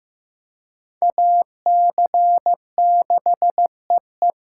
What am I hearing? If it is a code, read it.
Morse code, heard as a keyed tone, about 15 wpm, 700 Hz. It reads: AC6EE
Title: USA Call Signs - 5 Characters - 2x Character Spacing - 15wpm